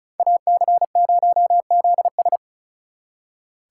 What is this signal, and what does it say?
Morse code, heard as a keyed tone, about 44 wpm, 700 Hz. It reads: AC0ZS